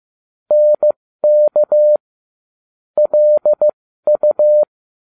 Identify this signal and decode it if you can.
Morse code, heard as a keyed tone, about 15 wpm, spaced as for 13 wpm, 600 Hz. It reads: NK LU